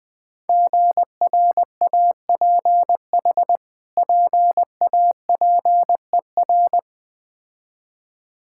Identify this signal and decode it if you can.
Morse code, heard as a keyed tone, about 21 wpm, 700 Hz. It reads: GRAPH PAPER
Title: Difficult Words with Hints - 20wpm